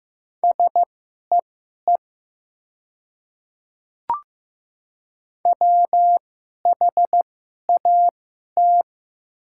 Morse code, heard as a keyed tone, about 15 wpm, spaced as for 11 wpm, 700 Hz. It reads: SEE WHAT
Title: 2x Character Spacing - Top 100 Words - 15wpm